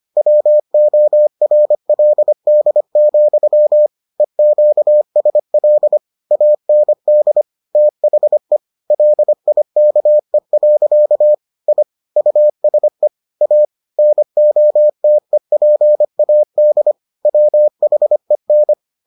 Morse code, heard as a keyed tone, about 25 wpm, 600 Hz. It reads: WORLD, EQSL AND THE LIKE. I USE A NOTEPAD WHEN